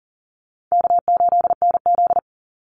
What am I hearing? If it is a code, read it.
Morse code, heard as a keyed tone, about 40 wpm, 700 Hz. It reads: K8NZ
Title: Call Signs - 40wpm